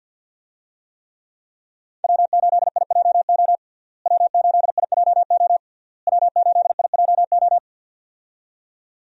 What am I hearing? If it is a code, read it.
Morse code, heard as a keyed tone, about 48 wpm, 700 Hz. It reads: W8IJO W8IJO W8IJO
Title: USA Call Signs - 5 Characters - Repeated 3x - 50wpm